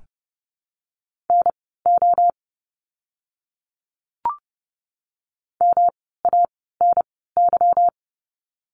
Morse code, heard as a keyed tone, about 30 wpm, spaced as for 17 wpm, 700 Hz. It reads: NO MANY
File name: course / 30:10wpm / 230 Lesson 37f - Letter x - Words 30wpm